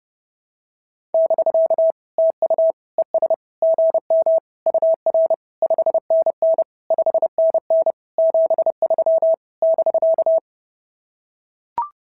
Morse code, heard as a keyed tone, about 30 wpm, 690 Hz. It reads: <BK> TU ES GM UR 5NN 5NN 73 <BK>
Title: SOTA QSO Practice 30wpm